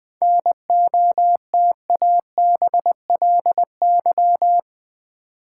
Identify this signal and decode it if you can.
Morse code, heard as a keyed tone, about 20 wpm, 690 Hz. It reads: NOTABLY
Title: 7 Letter Words - First 3 Letters Given - 20wpm